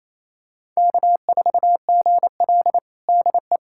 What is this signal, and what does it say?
Morse code, heard as a keyed tone, about 28 wpm, 700 Hz. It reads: K4GL DE
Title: USA Callsign DE Callsign - 4 Characters - 28wpm